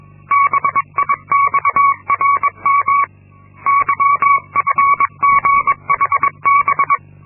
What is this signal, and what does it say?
Morse code, heard as a keyed tone, about 22 wpm, 1120 Hz. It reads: BIXRM YFGHB
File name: Linksmakalnis-2